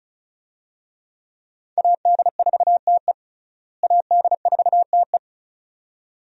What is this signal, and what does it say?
Morse code, heard as a keyed tone, about 34 wpm, 700 Hz. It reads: AD4TE AD4TE
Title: USA Call Signs - 5 Characters - Repeated 2x - 35wpm